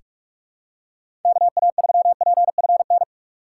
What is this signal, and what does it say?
Morse code, heard as a keyed tone, about 46 wpm, 700 Hz. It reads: KA3PFN
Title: USA Call Signs - 6 Characters - 6x Character Spacing - 45wpm